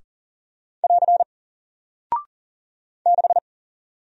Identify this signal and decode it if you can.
Morse code, heard as a keyed tone, about 40 wpm, 700 Hz. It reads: <AR> 6
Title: Single Character (Letter:Number:Punctuation:Prosign) Alt 40wpm